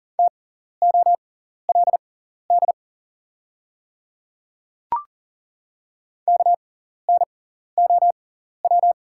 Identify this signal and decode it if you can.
Morse code, heard as a keyed tone, about 40 wpm, spaced as for 14 wpm, 690 Hz. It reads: TOLD KNOW
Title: 6x Character Spacing - Top 500 Words - 40wpm